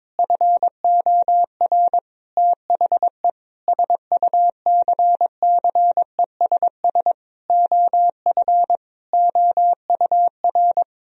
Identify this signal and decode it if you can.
Morse code, heard as a keyed tone, about 22 wpm, 700 Hz. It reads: FOR THE SUCCESS OF OUR
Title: Sentences from Top 1000 Words - Review Part 4 - 22wpm